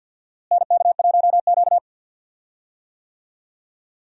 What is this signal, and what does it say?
Morse code, heard as a keyed tone, about 52 wpm, 700 Hz. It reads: NK1Q
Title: USA Call Signs - 4 Characters - Repeated 3x - 50wpm